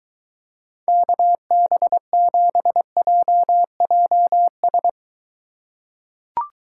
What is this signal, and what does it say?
Morse code, heard as a keyed tone, about 22 wpm, 700 Hz. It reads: KB7JJS